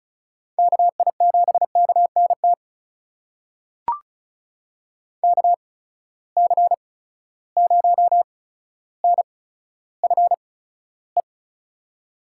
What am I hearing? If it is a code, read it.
Morse code, heard as a keyed tone, about 36 wpm, spaced as for 15 wpm, 700 Hz. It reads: KI7KNT KC0NFE
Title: USA Call Signs - 6 Characters - 8x Character Spacing - 35wpm